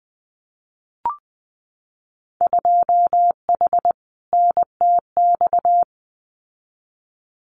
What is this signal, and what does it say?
Morse code, heard as a keyed tone, about 20 wpm, 700 Hz. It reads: 2H NTX
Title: Winter Field Day Exchanges Encore 20wpm